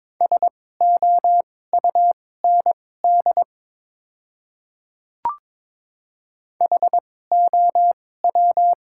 Morse code, heard as a keyed tone, about 22 wpm, spaced as for 16 wpm, 700 Hz. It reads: SOUND HOW
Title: 2x Character Spacing - Top 200 Words - 22wpm